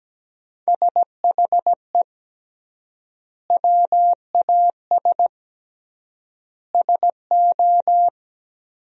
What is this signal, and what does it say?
Morse code, heard as a keyed tone, about 17 wpm, 700 Hz. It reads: SHE WAS SO
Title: 3x Word-Spacing - Sentences from Top 400 Words - 17wpm